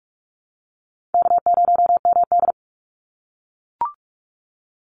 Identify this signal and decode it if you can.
Morse code, heard as a keyed tone, about 44 wpm, 700 Hz. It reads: K0MD
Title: Call Signs - 45wpm